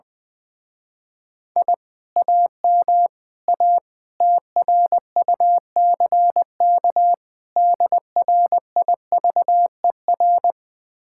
Morse code, heard as a keyed tone, about 20 wpm, 700 Hz. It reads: I AM A TRUCK DRIVER
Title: Speed-Racing Common QSO Phrases Encore - 20wpm